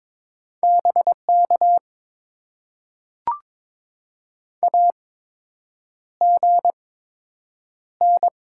Morse code, heard as a keyed tone, about 22 wpm, spaced as for 8 wpm, 700 Hz. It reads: BK AGN
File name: Top 90 QSO Elements - 8x Character Spacing - 22wpm